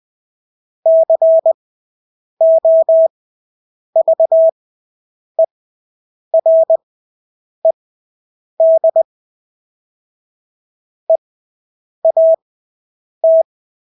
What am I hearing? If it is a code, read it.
Morse code, heard as a keyed tone, about 20 wpm, spaced as for 8 wpm, 650 Hz. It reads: COVERED EAT
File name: Engelske ord 20/8 wpm